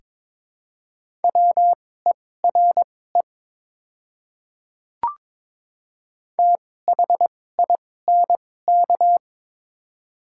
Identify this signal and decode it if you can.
Morse code, heard as a keyed tone, about 22 wpm, spaced as for 16 wpm, 700 Hz. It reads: WERE THINK
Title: 2x Character Spacing - Top 200 Words - 22wpm